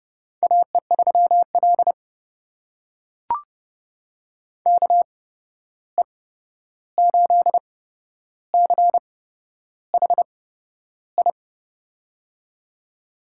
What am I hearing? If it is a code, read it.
Morse code, heard as a keyed tone, about 29 wpm, spaced as for 11 wpm, 700 Hz. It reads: AE3L KE8CHI
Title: USA Call Signs - 8x Character Spacing - 30wpm